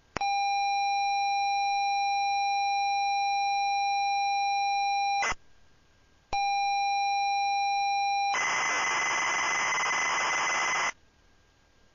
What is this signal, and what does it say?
ZX Spectrum tape loading.